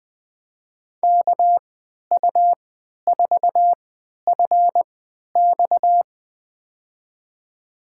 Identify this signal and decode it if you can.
Morse code, heard as a keyed tone, about 20 wpm, spaced as for 11 wpm, 700 Hz. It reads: KU4FX
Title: USA Call Signs - 5 Characters - 3x Character Spacing - 20wpm